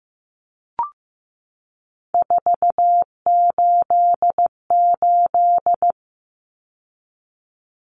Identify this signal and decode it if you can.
Morse code, heard as a keyed tone, about 15 wpm, 700 Hz. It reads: 488
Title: Numbers - 3-Digit 15wpm